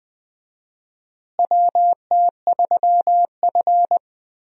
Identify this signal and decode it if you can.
Morse code, heard as a keyed tone, about 21 wpm, 700 Hz. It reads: WT3F
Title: USA Call Signs - 4 Characters - Repeated 3x - 20wpm